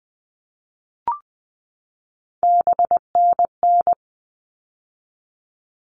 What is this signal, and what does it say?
Morse code, heard as a keyed tone, about 20 wpm, 700 Hz. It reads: BNN
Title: All Cut Numbers - 3-Digit 20wpm